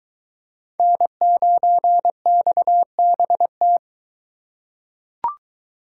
Morse code, heard as a keyed tone, about 21 wpm, 700 Hz. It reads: N9XBT